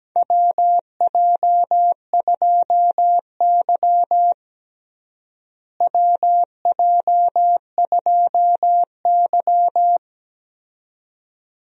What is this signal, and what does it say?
Morse code, heard as a keyed tone, about 17 wpm, 700 Hz. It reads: WJ2Y WJ2Y
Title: USA Call Signs - 4 Characters - Repeated 2x - 17wpm